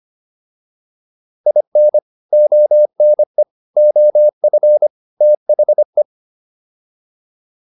Morse code, heard as a keyed tone, about 25 wpm, 600 Hz. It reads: IN ONE OF THE